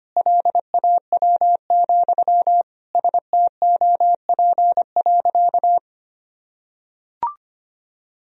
Morse code, heard as a keyed tone, about 25 wpm, 700 Hz. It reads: LAW, STOP.
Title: Sentences from Top 1000 Words - Review Part 3 - 25wpm